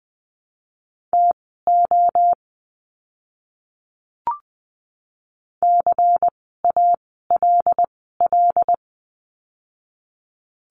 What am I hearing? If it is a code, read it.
Morse code, heard as a keyed tone, about 20 wpm, spaced as for 15 wpm, 700 Hz. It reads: TO CALL
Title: 160 Lesson 26f - Letter g - Words 20wpm